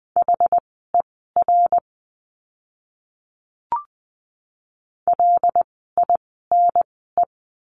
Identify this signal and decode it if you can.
Morse code, heard as a keyed tone, about 20 wpm, spaced as for 14 wpm, 700 Hz. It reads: HER LINE
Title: 115 Lesson 19f - Letter w - Words 20wpm